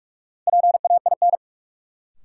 Morse code, heard as a keyed tone, about 46 wpm, 700 Hz. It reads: PAIN
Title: Sentences from Top 1600 Words - New Word Drill - 45wpm